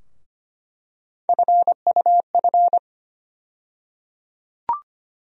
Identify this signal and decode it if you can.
Morse code, heard as a keyed tone, about 25 wpm, 700 Hz. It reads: FUF
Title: USA Call Sign Postfixes - 3 Letters 25wpm